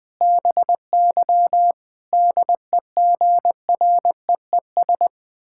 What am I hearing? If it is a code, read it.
Morse code, heard as a keyed tone, about 21 wpm, 700 Hz. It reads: BY DEGREES
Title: Pangram Challenge v2 - 20wpm